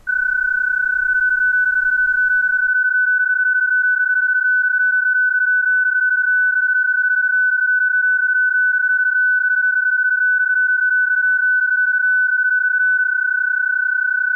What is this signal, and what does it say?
PSK31, about 31.25 baud.